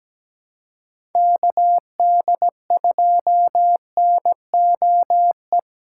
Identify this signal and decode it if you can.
Morse code, heard as a keyed tone, about 17 wpm, 700 Hz. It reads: KD2NOE
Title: USA Callsign DE Callsign - 6 Characters - 17wpm